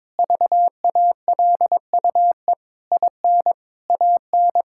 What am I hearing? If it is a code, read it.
Morse code, heard as a keyed tone, about 21 wpm, 700 Hz. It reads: VALUE IN AN
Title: Sentences from Top 1000 Words - Review Part 6 - 22wpm